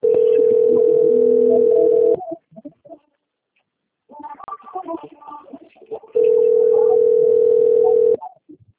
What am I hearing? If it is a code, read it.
Ringback tone (North America), 440 + 480 Hz.